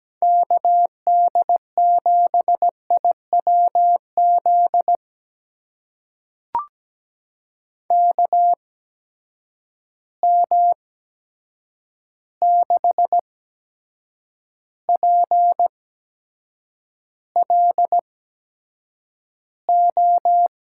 Morse code, heard as a keyed tone, about 17 wpm, spaced as for 7 wpm, 700 Hz. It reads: KD7IWZ KM6PLO
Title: USA Call Signs - 6 Characters - 8x Character Spacing - 17wpm